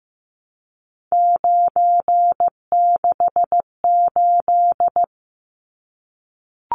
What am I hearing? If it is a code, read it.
Morse code, heard as a keyed tone, about 15 wpm, 700 Hz. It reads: 968
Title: Numbers - 3-Digit 15wpm